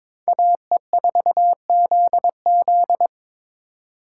Morse code, heard as a keyed tone, about 26 wpm, 700 Hz. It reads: AE4ZZ
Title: Speed Racing - USA 2x2 Call Signs 22wpm